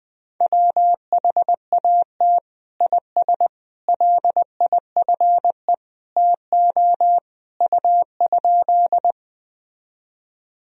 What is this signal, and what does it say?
Morse code, heard as a keyed tone, about 20 wpm, 700 Hz. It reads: WHAT IS LIFE TO U?
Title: Sentences from Top 300 Words and Abbreviations 20wpm